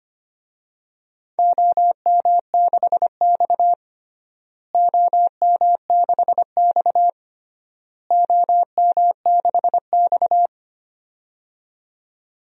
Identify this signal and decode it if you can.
Morse code, heard as a keyed tone, about 25 wpm, 700 Hz. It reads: OM6X OM6X OM6X
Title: DXCC-100 Call Signs - 4 Characters - Repeated 3x - 25wpm